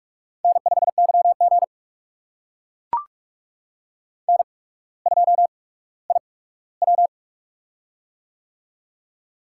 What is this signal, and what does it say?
Morse code, heard as a keyed tone, about 44 wpm, spaced as for 19 wpm, 700 Hz. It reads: N5YG N2IW
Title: USA Call Signs - 4 Characters - 8x Character Spacing - 45wpm